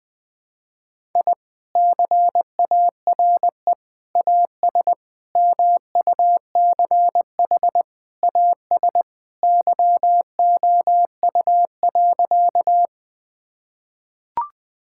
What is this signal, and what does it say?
Morse code, heard as a keyed tone, about 20 wpm, 700 Hz. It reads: I CARE AS MUCH AS YOU.
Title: Sentences with 3-Word Phrases - Spoken Beginning - 20wpm